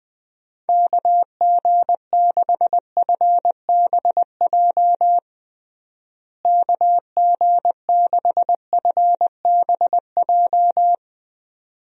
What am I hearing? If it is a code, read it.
Morse code, heard as a keyed tone, about 20 wpm, 700 Hz. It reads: KG6FBJ KG6FBJ
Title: USA Call Signs - 6 Characters - Repeated 2x - 20wpm